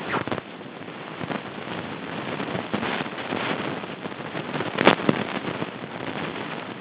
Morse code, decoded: none